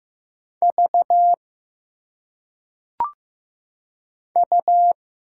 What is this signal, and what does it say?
Morse code, heard as a keyed tone, about 15 wpm, 700 Hz. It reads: V U
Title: Single Letters - Phonetic Pronunciation 15wpm